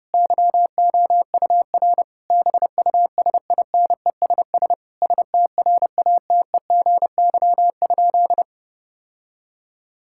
Morse code, heard as a keyed tone, about 32 wpm, 700 Hz. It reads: YOUR BUSINESS STRATEGY?
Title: Sentences from Top 1000 Words - Review Part 6 - 30wpm